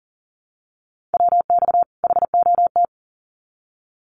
Morse code, heard as a keyed tone, about 40 wpm, 700 Hz. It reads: WX HOT
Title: Common QSO Phrases 40wpm